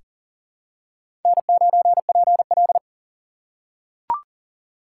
Morse code, heard as a keyed tone, about 40 wpm, 700 Hz. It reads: N9PL
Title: USA Call Signs - 4 Characters - 4x Character Spacing - 40wpm